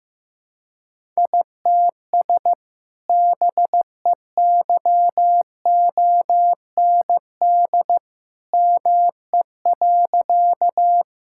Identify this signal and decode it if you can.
Morse code, heard as a keyed tone, about 15 wpm, 700 Hz. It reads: ITS BEYOND ME.
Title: Sentences from Top 1000 Words - Review Part 6 - 15wpm